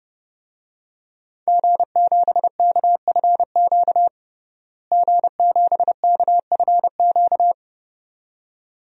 Morse code, heard as a keyed tone, about 30 wpm, 700 Hz. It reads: G7KFQ G7KFQ
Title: DXCC-100 Call Signs - 5 Characters - Repeated 2x - 30wpm